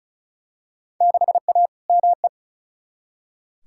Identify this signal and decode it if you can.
Morse code, heard as a keyed tone, about 36 wpm, 700 Hz. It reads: 6A ME
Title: Field Day Exchanges - 35wpm